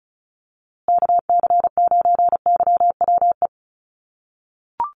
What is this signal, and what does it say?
Morse code, heard as a keyed tone, about 34 wpm, 700 Hz. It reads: KC9YWE
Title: USA Call Signs Repeated 3x 35wpm